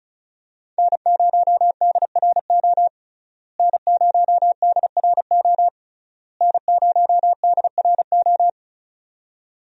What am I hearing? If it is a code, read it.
Morse code, heard as a keyed tone, about 34 wpm, 700 Hz. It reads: N0DRO N0DRO N0DRO
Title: USA Call Signs - 5 Characters - Repeated 3x - 35wpm